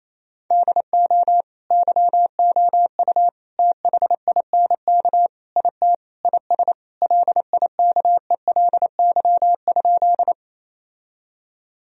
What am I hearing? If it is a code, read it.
Morse code, heard as a keyed tone, about 28 wpm, 700 Hz. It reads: DO YOU THINK IT IS LIKELY?
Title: Sentences from Top 1000 Words - Review Part 4 - 28wpm